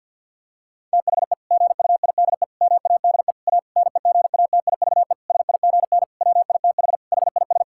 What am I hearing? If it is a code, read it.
Morse code, heard as a keyed tone, about 50 wpm, 700 Hz. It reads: THE GUIDE MADE A NEGATIVE SIGN WITH HIS
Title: Sentences from Top 1000 Words - Review Part 6 - 50wpm